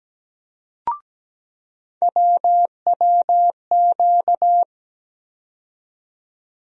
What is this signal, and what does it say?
Morse code, heard as a keyed tone, about 17 wpm, 700 Hz. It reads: WWQ
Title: USA Call Sign Postfixes - 3 Letters 17wpm